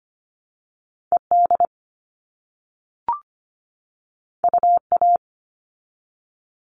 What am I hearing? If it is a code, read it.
Morse code, heard as a keyed tone, about 25 wpm, 700 Hz. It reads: ED UA
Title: All Cut Numbers - 2-Digit 25wpm